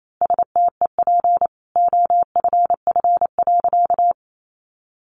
Morse code, heard as a keyed tone, about 28 wpm, 700 Hz. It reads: STEP OFF.
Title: A Sound Of Thunder 28wpm